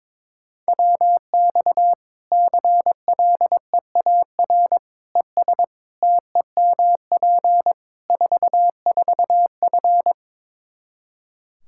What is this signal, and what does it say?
Morse code, heard as a keyed tone, about 22 wpm, 700 Hz. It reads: WX CLEAR ES TEMP 44F
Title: Practice QSO - Exchange 2 - Info - 22wpm